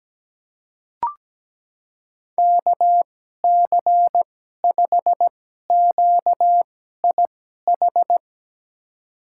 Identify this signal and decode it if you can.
Morse code, heard as a keyed tone, about 17 wpm, spaced as for 12 wpm, 700 Hz. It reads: KC5QIH
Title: USA Call Signs - 6 Characters - 2x Character Spacing - 17wpm